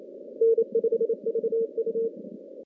Morse code, heard as a keyed tone, about 27 wpm, 400 Hz. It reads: N5VU
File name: contests / CW-SS-k5tr- -2100